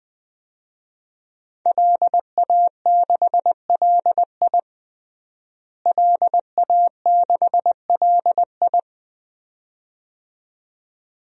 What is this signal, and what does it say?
Morse code, heard as a keyed tone, about 20 wpm, 700 Hz. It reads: LA6LI LA6LI
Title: DXCC-100 Call Signs - 5 Characters - Repeated 2x - 20wpm